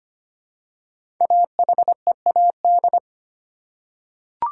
Morse code, heard as a keyed tone, about 25 wpm, 700 Hz. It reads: AHEAD
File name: 5 Letter Words - First 3 Letters Given - 25wpm